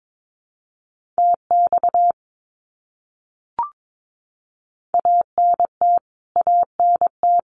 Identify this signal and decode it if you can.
Morse code, heard as a keyed tone, about 22 wpm, 700 Hz. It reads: TX ANT ANT
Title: Abbreviations Repeated 3x - 22wpm